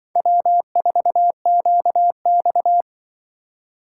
Morse code, heard as a keyed tone, about 25 wpm, 700 Hz. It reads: W4QX
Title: Speed Racing - USA 1x2 Call Signs 20wpm